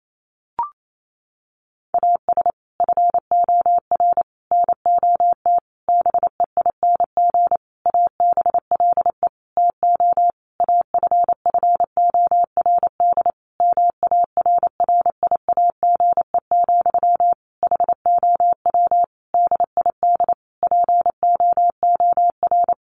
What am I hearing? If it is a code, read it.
Morse code, heard as a keyed tone, about 28 wpm, 700 Hz. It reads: AS FOR NOT BEING ABLE TO AFFORD MARRIAGE, HOW DID POOR
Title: Sentences from Top 1100 Words 28wpm